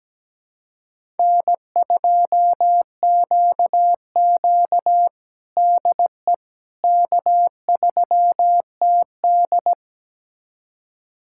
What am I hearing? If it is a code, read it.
Morse code, heard as a keyed tone, about 17 wpm, 700 Hz. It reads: N2QQ DE K3TD
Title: USA Callsign DE Callsign - 4 Characters - 17wpm